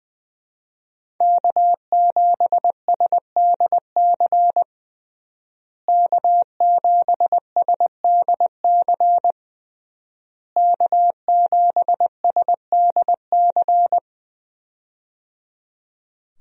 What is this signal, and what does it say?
Morse code, heard as a keyed tone, about 21 wpm, 700 Hz. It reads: K7SDC K7SDC K7SDC
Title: USA Call Signs - 5 Characters - Repeated 3x - 20wpm